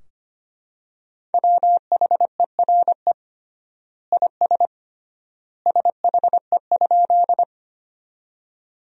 Morse code, heard as a keyed tone, about 25 wpm, 700 Hz. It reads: WHERE IS SHE?
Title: 3x Word-Spacing - Sentences from Top 200 Words - 25wpm